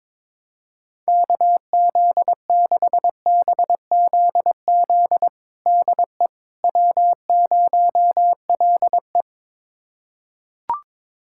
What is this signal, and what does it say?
Morse code, heard as a keyed tone, about 22 wpm, 700 Hz. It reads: KZ6BZZ DE W0LE
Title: USA Callsign DE Callsign 22wpm